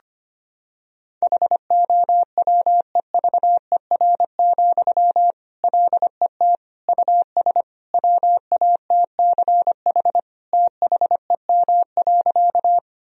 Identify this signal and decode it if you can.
Morse code, heard as a keyed tone, about 25 wpm, 700 Hz. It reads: HOWEVER, LET US WATCH THEM.
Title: Sentences from Top 1000 Words - Review Part 1 - 25wpm